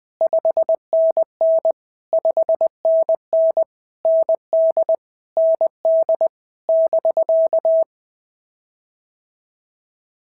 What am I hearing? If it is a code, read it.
Morse code, heard as a keyed tone, about 19 wpm, 650 Hz. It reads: 5NN 5NN ND ND <BK>
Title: POTA QSO Practice - 20wpm